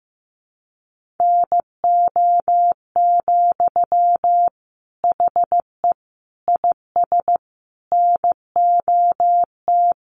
Morse code, heard as a keyed tone, about 15 wpm, 700 Hz. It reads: NO, HE IS NOT
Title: Sentences from Top 1400 Words 15wpm